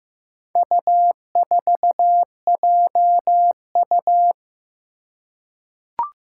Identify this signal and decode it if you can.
Morse code, heard as a keyed tone, about 15 wpm, 700 Hz. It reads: U4JU
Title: Four Challenging Letter-Number Combinations 15wpm